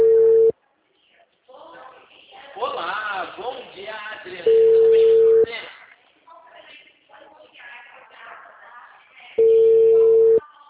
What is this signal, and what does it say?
Ringback tone (China, 450 Hz), with speech.